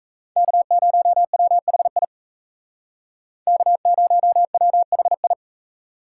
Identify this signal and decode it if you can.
Morse code, heard as a keyed tone, about 37 wpm, 700 Hz. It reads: K0WHI K0WHI